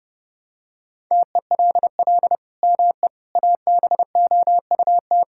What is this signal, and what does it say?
Morse code, heard as a keyed tone, about 32 wpm, 700 Hz. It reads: TELL ME ABOUT
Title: Sentences from Top 1000 Words - Review Part 6 - 30wpm